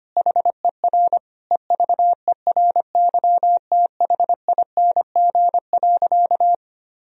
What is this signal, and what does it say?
Morse code, heard as a keyed tone, about 25 wpm, 700 Hz. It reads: HER EVERYTHING.
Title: Sentences from Top 1000 Words - Review Part 1 - 25wpm